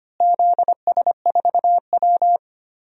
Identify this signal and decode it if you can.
Morse code, heard as a keyed tone, about 25 wpm, 700 Hz. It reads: ZS4W
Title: DXCC-100 Call Signs - 4 Characters - Repeated 3x - 25wpm